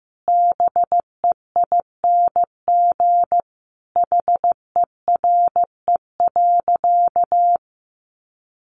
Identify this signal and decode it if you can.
Morse code, heard as a keyed tone, about 15 wpm, 700 Hz. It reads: BEING HERE.
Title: Sentences from Top 700 Words 15wpm